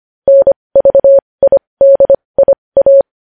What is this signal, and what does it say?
Morse code, heard as a keyed tone, about 25 wpm, spaced as for 20 wpm, 550 Hz. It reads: NVIDIA
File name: podcast / News headlines-1x-25x20